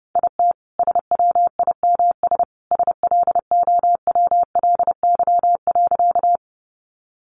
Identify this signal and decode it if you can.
Morse code, heard as a keyed tone, about 28 wpm, 700 Hz. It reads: IT SWIMS SLOWLY.
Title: Easy Phrases 30wpm